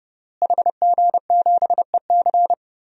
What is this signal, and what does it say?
Morse code, heard as a keyed tone, about 30 wpm, 700 Hz. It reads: HG7EC
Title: DXCC-100 Call Signs - 5 Characters - Repeated 3x - 30wpm